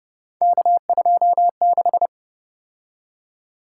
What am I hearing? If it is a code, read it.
Morse code, heard as a keyed tone, about 31 wpm, 700 Hz. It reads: K26